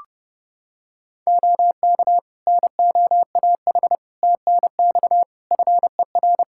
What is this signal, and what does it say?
Morse code, heard as a keyed tone, about 29 wpm, 700 Hz. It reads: OK NOAH TNX FER
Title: Practice QSO - Exchange 4 - Ending - 30wpm